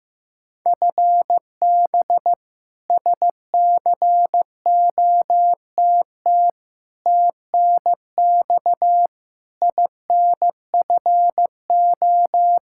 Morse code, heard as a keyed tone, about 15 wpm, 700 Hz. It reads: FB SCOTT TNX INFO
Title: K1USN SST QSO Practice 15wpm